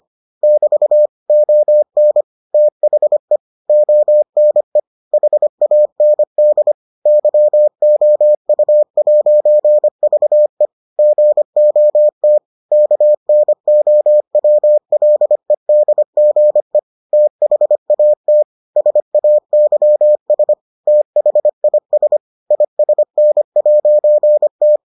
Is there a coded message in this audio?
Yes: Morse code from 0.4 to 24.8 s, heard as a keyed tone, about 25 wpm, 600 Hz. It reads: <BT> ON THE ONE HAND YOU'VE GOT KNOWLEDGE THAT SAYS THIS ISN'T